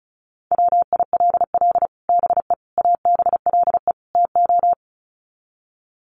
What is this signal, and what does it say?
Morse code, heard as a keyed tone, about 34 wpm, 700 Hz. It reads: WILL BE ABLE TO
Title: Sets of 4 Words 35wpm